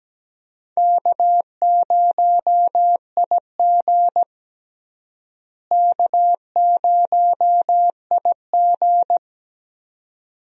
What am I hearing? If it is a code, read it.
Morse code, heard as a keyed tone, about 17 wpm, 700 Hz. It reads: K0IG K0IG
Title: USA Call Signs - 4 Characters - Repeated 2x - 17wpm